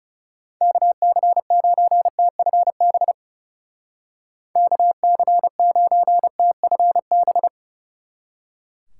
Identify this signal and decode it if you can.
Morse code, heard as a keyed tone, about 37 wpm, 700 Hz. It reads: KC9TFB KC9TFB
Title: Speed Racing - USA Call Signs 30wpm